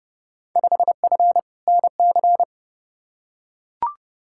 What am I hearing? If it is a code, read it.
Morse code, heard as a keyed tone, about 29 wpm, 700 Hz. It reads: 5F NC
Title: Field Day Exchanges Encore - 30wpm